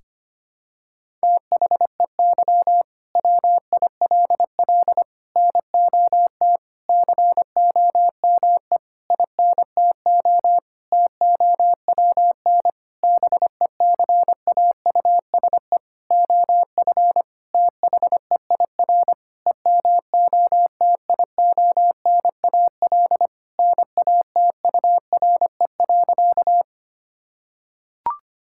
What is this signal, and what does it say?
Morse code, heard as a keyed tone, about 25 wpm, 700 Hz. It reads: THEY WILL NOT COME INTO TOWN BECAUSE OF THEIR EMOTIONAL NATURE.
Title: Sentences from Top 1000 Words - Review Part 5 - 25wpm